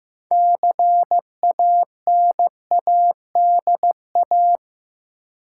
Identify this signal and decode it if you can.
Morse code, heard as a keyed tone, about 15 wpm, 700 Hz. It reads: CANADA
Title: Sets of 5 Words - New Word Drill - 15wpm